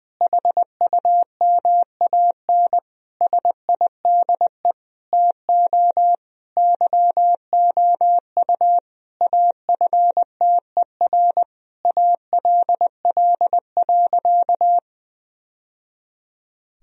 Morse code, heard as a keyed tone, about 21 wpm, 700 Hz. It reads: HUMAN SIDE TO YOU AFTER ALL.
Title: Sentences from Top 1000 Words - Review Part 3 - 20wpm